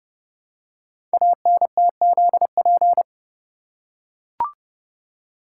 Morse code, heard as a keyed tone, about 30 wpm, 700 Hz. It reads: ANTZP
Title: Code Groups - 5 Letters - 30wpm